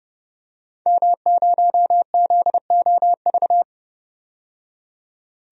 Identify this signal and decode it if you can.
Morse code, heard as a keyed tone, about 30 wpm, 700 Hz. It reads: M0ZOV
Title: DXCC-100 Call Signs - 5 Characters - 30wpm